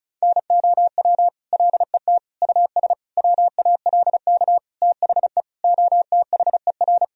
Morse code, heard as a keyed tone, about 34 wpm, 700 Hz. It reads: NOW LET US WALK THE OTHER
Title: Sentences from Top 1000 Words - Review Part 1 - 35wpm